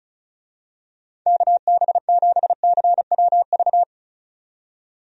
Morse code, heard as a keyed tone, about 35 wpm, 700 Hz. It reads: KB7CWV
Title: USA Call Signs - 6 Characters - 35wpm